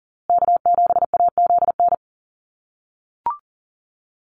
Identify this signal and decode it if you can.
Morse code, heard as a keyed tone, about 40 wpm, 700 Hz. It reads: K7AZN
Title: USA Call Signs Repeated 2x 40wpm